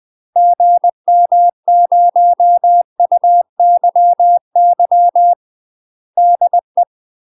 Morse code, heard as a keyed tone, about 20 wpm, 700 Hz. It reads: GM0UYY DE
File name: UK MASTER QSOs TRAINING / 20WPM / UK MASTER QSOs 0049